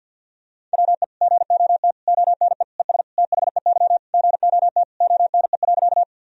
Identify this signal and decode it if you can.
Morse code, heard as a keyed tone, about 51 wpm, 700 Hz. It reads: WE GOT ONE ES THEY GOT ONE.
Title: Sentences from Top 300 Words and Abbreviations 50wpm